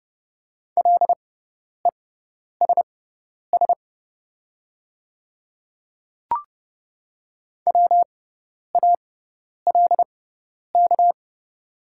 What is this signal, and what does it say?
Morse code, heard as a keyed tone, about 30 wpm, spaced as for 10 wpm, 700 Hz. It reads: LESS WALK
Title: 6x Character Spacing - Top 500 Words - 30wpm